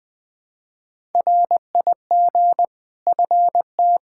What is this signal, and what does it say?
Morse code, heard as a keyed tone, about 20 wpm, 700 Hz. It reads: RIG FT
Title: K1USN SST QSO Practice 20wpm